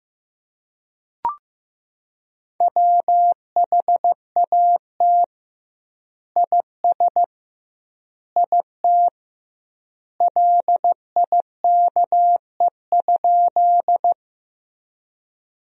Morse code, heard as a keyed tone, about 15 wpm, 700 Hz. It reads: WHAT IS IT LIKE?
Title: 2x Word-Spacing - Sentences from Top 100 Words - 15wpm